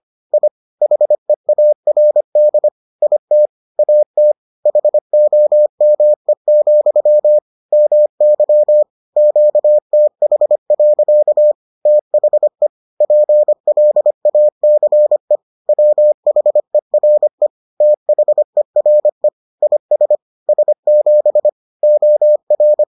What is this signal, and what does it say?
Morse code, heard as a keyed tone, about 25 wpm, 600 Hz. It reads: I HEARD IT AT HOME, MY QTH. THE PLACE WHERE THERE IS S7 OR